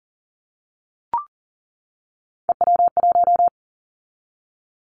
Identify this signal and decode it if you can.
Morse code, heard as a keyed tone, about 41 wpm, 700 Hz. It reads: EW1
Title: Callsign Prefixes for Top 50 DXCC - 40wpm